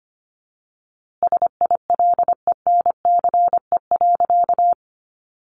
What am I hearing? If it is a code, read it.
Morse code, heard as a keyed tone, about 25 wpm, 700 Hz. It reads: SILENCE.
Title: A Sound Of Thunder 25wpm